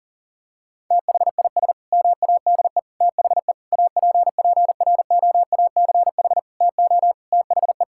Morse code, heard as a keyed tone, about 37 wpm, 700 Hz. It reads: THIS MADE THE APPROACH TO THE
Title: Sentences from Top 1000 Words - Review Part 6 - 40wpm